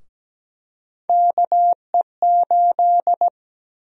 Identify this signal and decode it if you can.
Morse code, heard as a keyed tone, about 17 wpm, 700 Hz. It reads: KE8
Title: USA Call Sign Prefixes - 17wpm